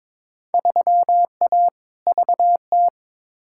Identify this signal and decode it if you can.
Morse code, heard as a keyed tone, about 22 wpm, 700 Hz. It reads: 3A VT
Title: Field Day Exchanges Encore - 22wpm